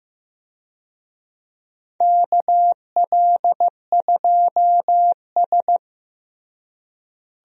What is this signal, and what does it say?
Morse code, heard as a keyed tone, about 15 wpm, 700 Hz. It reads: KL2S
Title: USA Call Signs - 4 Characters - Repeated 2x - 15wpm